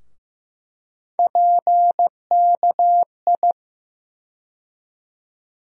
Morse code, heard as a keyed tone, about 15 wpm, 700 Hz. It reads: PKI
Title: USA Call Sign Postfixes - 3 Letters 15wpm